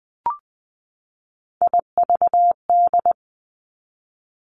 Morse code, heard as a keyed tone, about 20 wpm, 700 Hz. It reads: IVD
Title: Three Challenging Letter-Number Combinations 20wpm